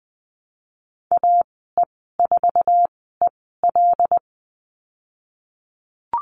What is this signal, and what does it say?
Morse code, heard as a keyed tone, about 19 wpm, spaced as for 14 wpm, 700 Hz. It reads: AE4EL
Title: 122 Lesson 20g - Number 3 - Call Signs 20wpm